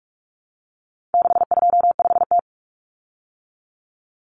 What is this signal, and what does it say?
Morse code, heard as a keyed tone, about 44 wpm, 700 Hz. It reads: 625T
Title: Common Cut Numbers - 4-Digit 45wpm